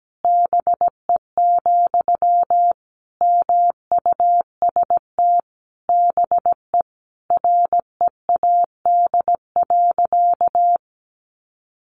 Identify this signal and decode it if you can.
Morse code, heard as a keyed tone, about 17 wpm, 700 Hz. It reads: BE, MUST BE READ.
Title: Sentences from Top 1100 Words 17wpm